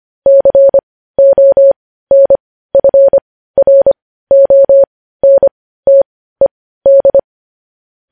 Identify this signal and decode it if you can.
Morse code, heard as a keyed tone, about 25 wpm, spaced as for 15 wpm, 550 Hz. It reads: CONFRONTED
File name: podcast / News headlines-1x-25x15